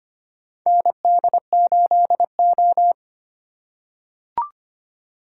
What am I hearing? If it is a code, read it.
Morse code, heard as a keyed tone, about 25 wpm, 700 Hz. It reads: ND8O
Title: USA Call Signs - 4 Characters - 4x Character Spacing - 25wpm